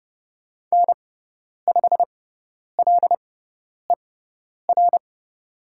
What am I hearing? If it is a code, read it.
Morse code, heard as a keyed tone, about 30 wpm, spaced as for 10 wpm, 700 Hz. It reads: N5LER
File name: Lesson 18g - Letter m - Call Signs 30wpm